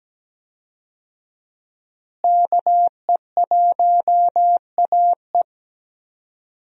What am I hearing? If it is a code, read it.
Morse code, heard as a keyed tone, about 17 wpm, 700 Hz. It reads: KE1AE
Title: USA Call Signs - 5 Characters - Repeated 2x - 17wpm